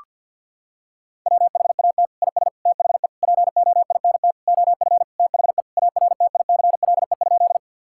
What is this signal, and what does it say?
Morse code, heard as a keyed tone, about 50 wpm, 700 Hz. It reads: WHAT IS THE POINT OF THE ARTICLE?
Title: Sentences from Top 1000 Words - Review Part 5 - 50wpm